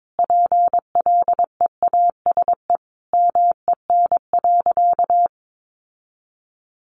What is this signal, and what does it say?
Morse code, heard as a keyed tone, about 22 wpm, 700 Hz. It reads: PLEASE MEN.
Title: Sentences from Top 1500 Words 22wpm